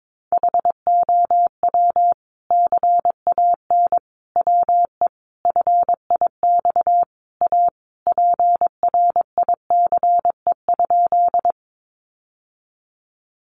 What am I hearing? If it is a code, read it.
Morse code, heard as a keyed tone, about 22 wpm, 700 Hz. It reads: HOW CAN WE FIX A PRICE?
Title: Sentences from Top 1400 Words 22wpm